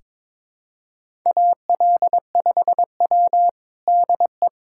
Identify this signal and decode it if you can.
Morse code, heard as a keyed tone, about 22 wpm, 700 Hz. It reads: AL5W DE
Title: USA Callsign DE Callsign - 4 Characters - 22wpm